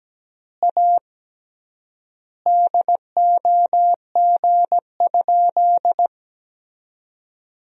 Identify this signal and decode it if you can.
Morse code, heard as a keyed tone, about 17 wpm, 700 Hz. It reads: A DOG?
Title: 3x Word-Spacing - Sentences from Top 400 Words - 17wpm